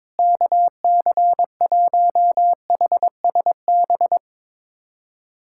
Morse code, heard as a keyed tone, about 22 wpm, 700 Hz. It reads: KC1HSB
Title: USA Callsign DE Callsign - 6 Characters - 22wpm